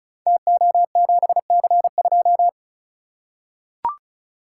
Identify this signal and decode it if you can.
Morse code, heard as a keyed tone, about 36 wpm, 700 Hz. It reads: TO7C2
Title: Code Groups - 5 Letter:Numbers - 35wpm